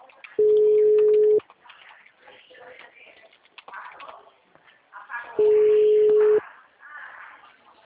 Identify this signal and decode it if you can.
Ringback tone (Europe, 425 Hz), with speech.